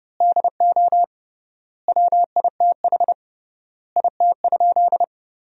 Morse code, heard as a keyed tone, about 30 wpm, 700 Hz. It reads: DO WITH IT?
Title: 3x Word-Spacing - Sentences from Top 100 Words - 30wpm